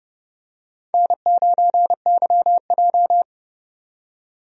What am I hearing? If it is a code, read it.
Morse code, heard as a keyed tone, about 31 wpm, 700 Hz. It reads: N9YJ
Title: USA Call Signs - 4 Characters - 30wpm